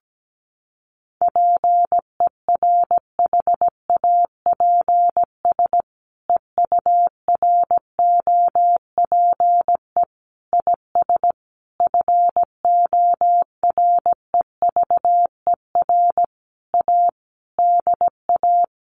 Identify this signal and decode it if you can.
Morse code, heard as a keyed tone, about 17 wpm, 700 Hz. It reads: PERHAPS EUROPE IS FOREVER A DA
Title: A Sound Of Thunder 17wpm